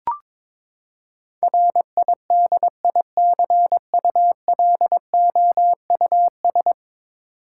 Morse code, heard as a keyed tone, about 21 wpm, 700 Hz. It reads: RIDICULOUS
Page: morse-code-ninja-practice-22wpm